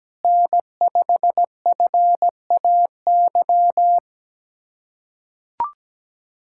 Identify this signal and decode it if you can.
Morse code, heard as a keyed tone, about 17 wpm, 700 Hz. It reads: N5FAY
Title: USA Callsign DE Callsign - 5 Characters - 17wpm